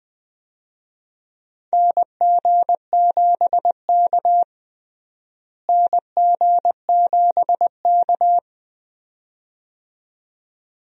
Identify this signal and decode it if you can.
Morse code, heard as a keyed tone, about 20 wpm, 700 Hz. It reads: NG7K NG7K
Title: USA Call Signs - 4 Characters - Repeated 2x - 20wpm